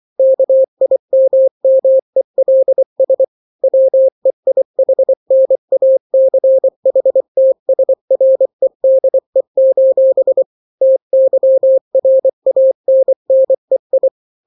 Morse code, heard as a keyed tone, about 24 wpm, 530 Hz. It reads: KIMMELS WEIHNACHTSREDE: TYRANNEI
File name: daily / 23wpm